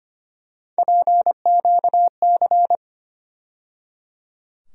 Morse code, heard as a keyed tone, about 25 wpm, 700 Hz. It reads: PQC
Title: USA Call Sign Postfixes - 3 Letters 25wpm